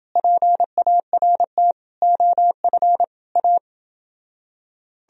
Morse code, heard as a keyed tone, about 29 wpm, 700 Hz. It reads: PART OF A